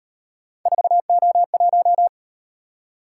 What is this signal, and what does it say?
Morse code, heard as a keyed tone, about 36 wpm, 700 Hz. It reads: 4O1